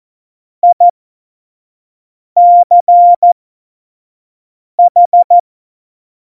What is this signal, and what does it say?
Morse code, heard as a keyed tone, about 14 wpm, spaced as for 5 wpm, 700 Hz. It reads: ICH
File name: Morsekurs / Witze / Witz 12. Dezember 2017, 25 F70